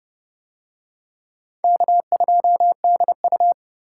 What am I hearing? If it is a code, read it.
Morse code, heard as a keyed tone, about 26 wpm, 700 Hz. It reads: K2DU